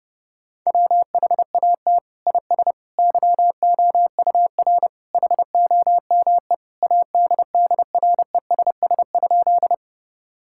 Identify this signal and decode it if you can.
Morse code, heard as a keyed tone, about 32 wpm, 700 Hz. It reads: WHAT IS YOUR HOME ADDRESS?
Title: Sentences from Top 1000 Words - Review Part 6 - 30wpm